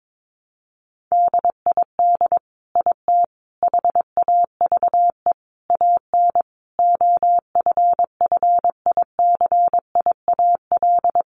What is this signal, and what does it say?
Morse code, heard as a keyed tone, about 22 wpm, 700 Hz. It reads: DID IT HAVE AN OFFICIAL
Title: Sentences from Top 1200 Words 22wpm